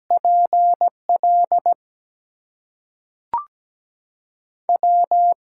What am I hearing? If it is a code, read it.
Morse code, heard as a keyed tone, about 17 wpm, spaced as for 5 wpm, 700 Hz. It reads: PLW
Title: USA Call Signs - 6 Characters - 6x Character Spacing - 17wpm